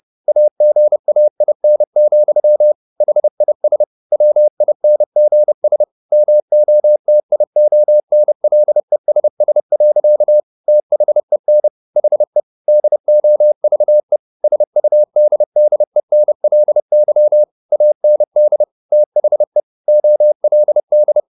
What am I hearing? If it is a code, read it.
Morse code, heard as a keyed tone, about 30 wpm, 600 Hz. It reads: AGAIN, HIS WINGS MOTIONLESS. THEN HE DOVE SUDDENLY AND THE OLD
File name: the old man and the sea / the old man and the sea 1 30wpm